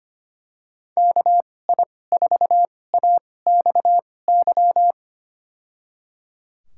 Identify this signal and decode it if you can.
Morse code, heard as a keyed tone, about 25 wpm, spaced as for 18 wpm, 700 Hz. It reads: KI4AXY
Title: USA Call Signs - 6 Characters - 2x Character Spacing - 25wpm